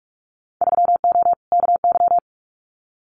Morse code, heard as a keyed tone, about 44 wpm, 700 Hz. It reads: 3O KY
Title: Winter Field Day Exchanges 45wpm